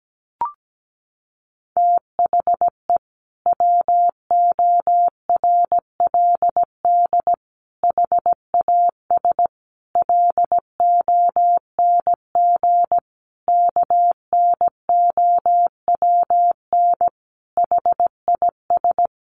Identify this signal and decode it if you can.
Morse code, heard as a keyed tone, about 17 wpm, 700 Hz. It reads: THE WORLD HAS LONG KNOWN HIS
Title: Sentences from Top 1200 Words 17wpm